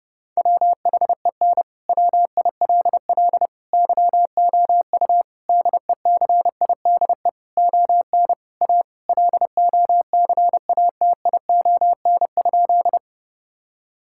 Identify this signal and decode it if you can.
Morse code, heard as a keyed tone, about 32 wpm, 700 Hz. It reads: WHEN WILL YOU DECIDE ON A LOCATION?
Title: Sentences from Top 1000 Words - Review Part 6 - 30wpm